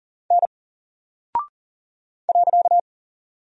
Morse code, heard as a keyed tone, about 40 wpm, 700 Hz. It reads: N .
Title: Single Character (Letter:Number:Punctuation:Prosign) Alt 40wpm